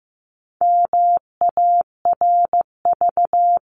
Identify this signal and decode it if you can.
Morse code, heard as a keyed tone, about 15 wpm, 700 Hz. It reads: MARV
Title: CWT Exchange 15wpm